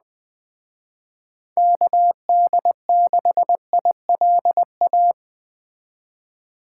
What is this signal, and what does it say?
Morse code, heard as a keyed tone, about 20 wpm, 700 Hz. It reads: KD6ILA